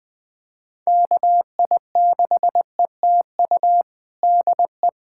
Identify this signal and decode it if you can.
Morse code, heard as a keyed tone, about 21 wpm, 700 Hz. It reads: KI6ETU DE
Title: USA Callsign DE Callsign - 6 Characters - 20wpm